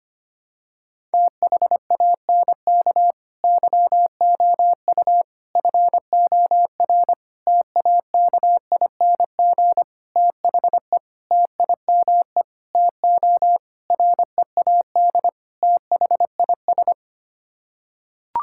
Morse code, heard as a keyed tone, about 25 wpm, 700 Hz. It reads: THANK YOU FOR TAKING THE TIME TO READ THIS
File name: Sets of 9 Words - 25wpm